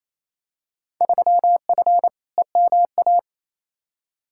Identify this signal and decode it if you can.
Morse code, heard as a keyed tone, about 28 wpm, 700 Hz. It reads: 3F EMA
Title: Field Day Exchanges Encore - 28wpm